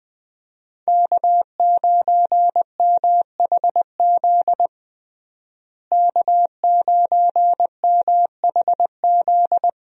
Morse code, heard as a keyed tone, about 20 wpm, 700 Hz. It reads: K9MHZ K9MHZ
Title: USA Call Signs - 5 Characters - Repeated 2x - 20wpm